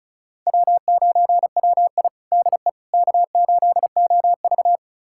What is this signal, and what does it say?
Morse code, heard as a keyed tone, about 36 wpm, 700 Hz. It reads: W9WI DE K8OV
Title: USA Callsign DE Callsign - 4 Characters - 35wpm